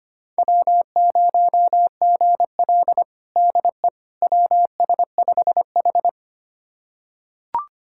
Morse code, heard as a keyed tone, about 25 wpm, 700 Hz. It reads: W0GL DE WS5H
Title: USA Callsign DE Callsign - 4 Characters - 25wpm